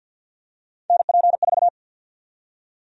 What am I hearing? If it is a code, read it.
Morse code, heard as a keyed tone, about 52 wpm, 700 Hz. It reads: NP4
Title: USA Call Sign Prefixes - Repeated 2x - 50wpm